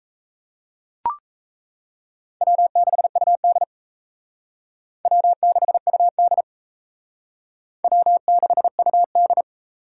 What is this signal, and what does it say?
Morse code, heard as a keyed tone, about 36 wpm, 700 Hz. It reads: W6UD W6UD W6UD